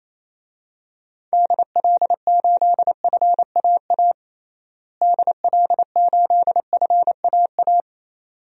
Morse code, heard as a keyed tone, about 27 wpm, 700 Hz. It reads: DL8FAA DL8FAA
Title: DXCC-100 Call Signs - 6 Characters - Repeated 2x - 28wpm